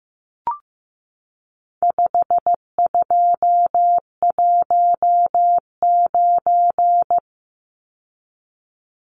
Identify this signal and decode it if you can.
Morse code, heard as a keyed tone, about 15 wpm, 700 Hz. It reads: 5219
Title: Numbers - 4-Digit 15wpm